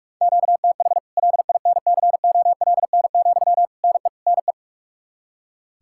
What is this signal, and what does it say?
Morse code, heard as a keyed tone, about 44 wpm, 700 Hz. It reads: QTH LINCOLN, NE NE
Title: Practice QSO - Exchange 1 - Beginning - 45wpm